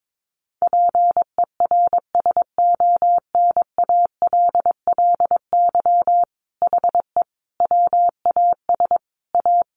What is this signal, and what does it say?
Morse code, heard as a keyed tone, about 22 wpm, 700 Hz. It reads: PERSONALLY HE WAS A
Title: Sentences from Top 1400 Words 22wpm